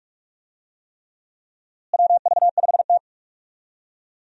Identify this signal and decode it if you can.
Morse code, heard as a keyed tone, about 46 wpm, 700 Hz. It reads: WV5T
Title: USA Call Signs - 4 Characters - Repeated 3x - 45wpm